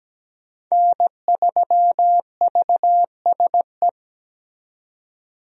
Morse code, heard as a keyed tone, about 17 wpm, 700 Hz. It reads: N3VSE
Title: USA Call Signs - 5 Characters - 17wpm